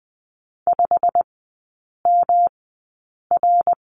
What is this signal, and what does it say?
Morse code, heard as a keyed tone, about 20 wpm, spaced as for 8 wpm, 700 Hz. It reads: 5MR
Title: 138 Lesson 23e - Letter f - Three Characters 20wpm